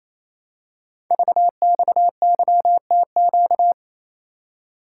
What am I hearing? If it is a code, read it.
Morse code, heard as a keyed tone, about 28 wpm, 700 Hz. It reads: VXYTQ
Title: Code Groups - 5 Letter:Numbers - 28wpm